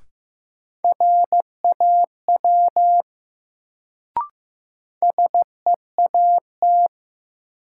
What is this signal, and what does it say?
Morse code, heard as a keyed tone, about 15 wpm, 700 Hz. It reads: RAW SEAT
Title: Sentences from Top 1500 Words - New Word Drill Alt - 15wpm